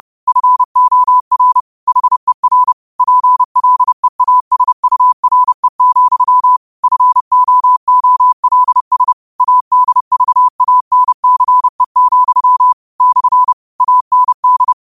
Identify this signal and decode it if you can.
Morse code, heard as a keyed tone, about 30 wpm, 1000 Hz. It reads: FOR HER PLEASURE, FOOLS ADVANCE, / AND
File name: qotdfiles / 30wpm